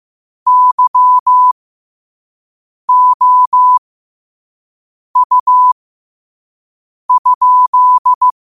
Morse code, heard as a keyed tone, about 15 wpm, spaced as for 5 wpm, 1000 Hz. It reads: YOU?
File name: qotdfiles / 5wpm